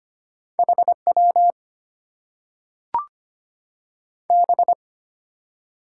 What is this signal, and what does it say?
Morse code, heard as a keyed tone, about 25 wpm, 700 Hz. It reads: HW B
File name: Top 90 QSO Elements - 8x Character Spacing - 25wpm